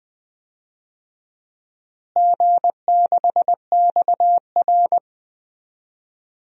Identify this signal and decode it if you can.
Morse code, heard as a keyed tone, about 21 wpm, 700 Hz. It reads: G6XR
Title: DXCC-100 Call Signs - 4 Characters - Repeated 3x - 20wpm